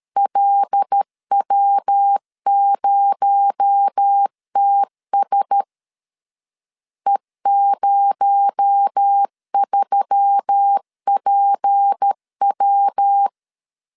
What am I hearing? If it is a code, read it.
Morse code, heard as a keyed tone, about 13 wpm, 800 Hz. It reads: LW0TS E03PW
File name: LEZIONI / 6 PKX0 / 6 PKX0 12wpm 2231